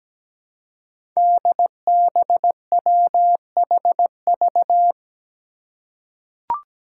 Morse code, heard as a keyed tone, about 17 wpm, 700 Hz. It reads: DBWHV
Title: Five Challenging Letter-Number Combinations 17wpm